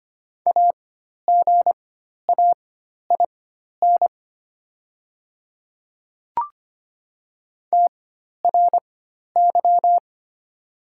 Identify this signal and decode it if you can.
Morse code, heard as a keyed tone, about 25 wpm, spaced as for 12 wpm, 690 Hz. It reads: AGAIN TRY
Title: 4x Character Spacing - Top 300 Words - 25wpm